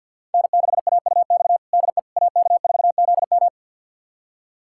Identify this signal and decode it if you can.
Morse code, heard as a keyed tone, about 47 wpm, 700 Hz. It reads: N6AUX DE AK4ZM
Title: USA Callsign DE Callsign - 5 Characters - 50wpm